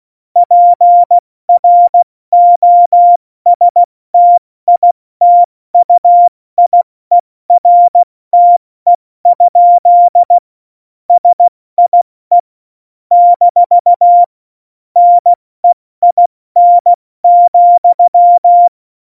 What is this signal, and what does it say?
Morse code, heard as a keyed tone, about 16 wpm, 700 Hz. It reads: PROSTITUIERTE? SIE - NEIN,
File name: Morsekurs / Witze / Witz 19. Januar 2021, 70 F80